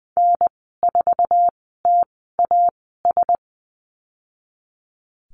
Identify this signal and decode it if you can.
Morse code, heard as a keyed tone, about 19 wpm, spaced as for 14 wpm, 700 Hz. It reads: N4TAS
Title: 048 Lesson 9g - Number 4 - Call Signs 20wpm